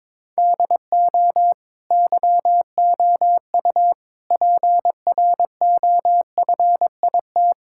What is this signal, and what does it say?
Morse code, heard as a keyed tone, about 22 wpm, 700 Hz. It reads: DO YOU PROFIT
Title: Sentences from Top 1000 Words - Review Part 5 - 22wpm